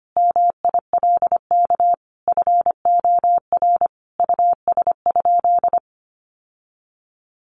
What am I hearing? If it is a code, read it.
Morse code, heard as a keyed tone, about 25 wpm, 700 Hz. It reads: MILK FOR US?
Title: Sentences from Top 1500 Words 25wpm